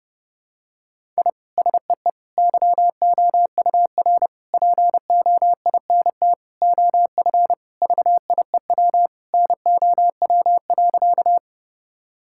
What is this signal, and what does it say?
Morse code, heard as a keyed tone, about 30 wpm, 700 Hz. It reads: I SEE YOUR POINT OF VIEW NOW.
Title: Sentences from Top 1000 Words - Review Part 6 - 30wpm